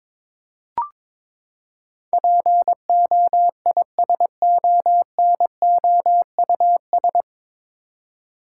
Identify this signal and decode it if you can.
Morse code, heard as a keyed tone, about 22 wpm, 700 Hz. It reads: POISONOUS
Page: Morse Code Ninja practice 22wpm